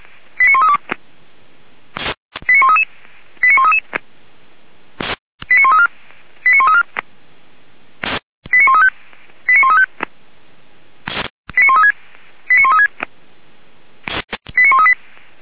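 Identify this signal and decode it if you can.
Five-tone selective call (ZVEI), heard as quick runs of five tones, tones of about 70 ms. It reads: 89132 89133 89133 89134 89134 89135 89135 89136 89136 89138
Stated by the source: ZVEI five-tone selective call received on an SDR.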